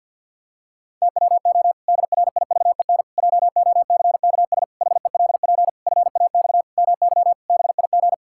Morse code, heard as a keyed tone, about 50 wpm, 700 Hz. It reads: TWO DRIVEN JOCKS HELP FAX MY BIG
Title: Pangram Challenge v2 - 50wpm